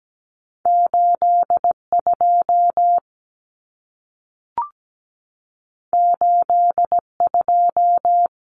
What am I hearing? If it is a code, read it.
Morse code, heard as a keyed tone, about 17 wpm, 700 Hz. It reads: 82 82
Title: Common Cut Numbers - 2-Digit 17wpm